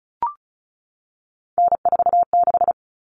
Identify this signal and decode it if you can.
Morse code, heard as a keyed tone, about 34 wpm, 700 Hz. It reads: N46
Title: Common Cut Numbers - 3-Digit 35wpm